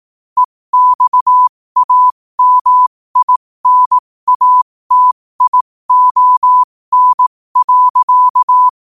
Morse code, heard as a keyed tone, about 18 wpm, spaced as for 15 wpm, 1000 Hz. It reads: EXAMINATION.
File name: qotdfiles / 15wpm